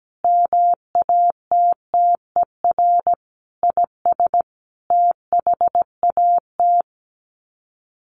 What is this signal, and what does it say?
Morse code, heard as a keyed tone, about 17 wpm, 700 Hz. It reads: MATTER IS THAT
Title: Sets of 5 Words Encore 17wpm